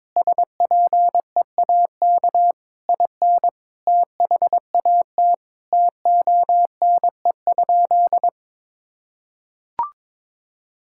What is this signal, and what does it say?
Morse code, heard as a keyed tone, about 22 wpm, 700 Hz. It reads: SPEAK IN THAT TONE?
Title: Sentences from Top 1600 Words 22wpm